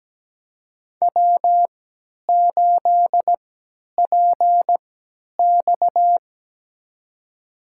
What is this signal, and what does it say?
Morse code, heard as a keyed tone, about 17 wpm, spaced as for 10 wpm, 700 Hz. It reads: W8PX
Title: USA Call Signs - 4 Characters - 3x Character Spacing - 17wpm